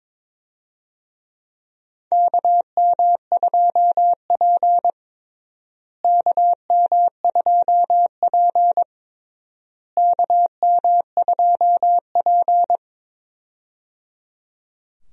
Morse code, heard as a keyed tone, about 22 wpm, 700 Hz. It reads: KM2P KM2P KM2P
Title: USA Call Signs - 4 Characters - Repeated 3x - 22wpm